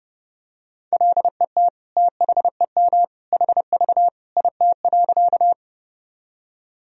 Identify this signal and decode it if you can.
Morse code, heard as a keyed tone, about 31 wpm, 700 Hz. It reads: LET THEM HV IT.
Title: Sentences from Top 300 Words and Abbreviations 30wpm